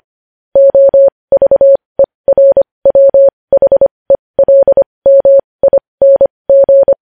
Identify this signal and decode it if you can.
Morse code, heard as a keyed tone, about 25 wpm, spaced as for 20 wpm, 550 Hz. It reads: OVERWHELMING
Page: news-headlines-in-morse-code-at-20-wpm